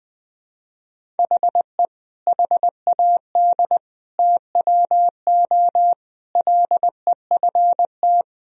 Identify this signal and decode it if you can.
Morse code, heard as a keyed tone, about 21 wpm, 700 Hz. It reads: HE HAD TWO LEFT
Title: Sentences from Top 1000 Words - Review Part 1 - 20wpm